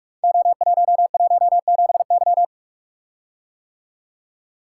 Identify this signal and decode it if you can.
Morse code, heard as a keyed tone, about 44 wpm, 690 Hz. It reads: O117Y
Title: Code Groups - 5 Letter:Numbers - 45wpm